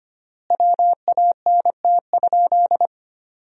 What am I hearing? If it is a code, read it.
Morse code, heard as a keyed tone, about 25 wpm, 700 Hz. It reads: WANT?
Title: 3x Word-Spacing - Sentences from Top 200 Words - 25wpm